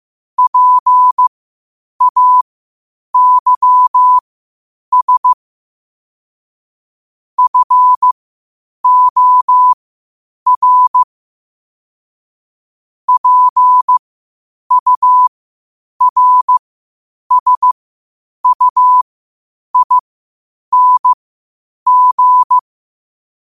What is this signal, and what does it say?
Morse code, heard as a keyed tone, about 15 wpm, spaced as for 8 wpm, 1000 Hz. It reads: PAYS FOR PURSUING
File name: qotdfiles / 8wpm